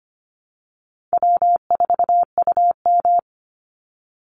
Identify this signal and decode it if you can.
Morse code, heard as a keyed tone, about 25 wpm, 700 Hz. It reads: W4UM
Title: Easy Call Signs - 25wpm